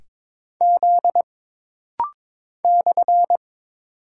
Morse code, heard as a keyed tone, about 22 wpm, 700 Hz. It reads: Z /
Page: This rapid-fire timing is geared to push your ICR (Instant Character Recognition) skills.